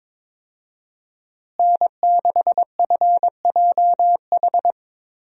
Morse code, heard as a keyed tone, about 21 wpm, 700 Hz. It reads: N6FJH
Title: USA Call Signs - 5 Characters - Repeated 3x - 22wpm